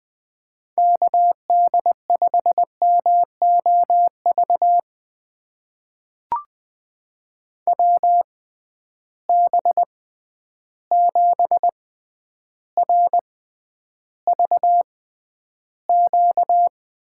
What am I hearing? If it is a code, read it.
Morse code, heard as a keyed tone, about 20 wpm, spaced as for 10 wpm, 700 Hz. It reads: KD5MOV WB7RVQ
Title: USA Call Signs - 6 Characters - 6x Character Spacing - 20wpm